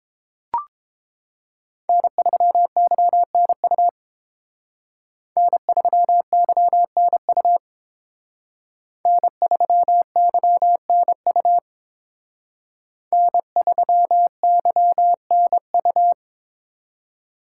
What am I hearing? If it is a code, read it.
Morse code, heard as a keyed tone, about 31 wpm, 700 Hz. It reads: N3YNU N3YNU N3YNU N3YNU